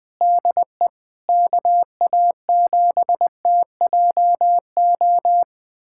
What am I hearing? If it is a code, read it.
Morse code, heard as a keyed tone, about 21 wpm, 700 Hz. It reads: DE KA7TJO
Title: USA Callsign DE Callsign - 6 Characters - 20wpm